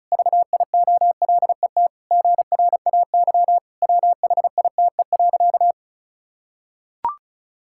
Morse code, heard as a keyed tone, about 34 wpm, 700 Hz. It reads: VIOLET GRAY WHITE.
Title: Pangram Challenge v2 - 35wpm